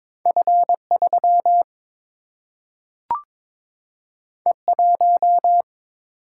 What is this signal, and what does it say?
Morse code, heard as a keyed tone, about 22 wpm, 700 Hz. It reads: F 3 E 1
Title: Single-Letter Single-Number 22wpm